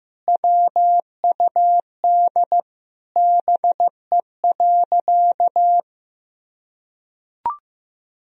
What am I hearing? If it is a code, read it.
Morse code, heard as a keyed tone, about 15 wpm, 700 Hz. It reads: WUD BE.
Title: Sentences from Top 200 Words and Abbreviations 15wpm